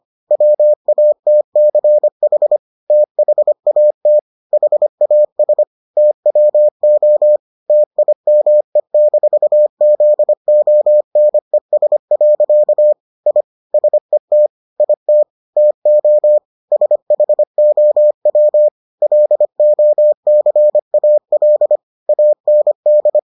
Morse code, heard as a keyed tone, about 25 wpm, 600 Hz. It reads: WATCH THAT HAS TWO TIME-ZONES. I SET IT TO SHOW LOCAL AND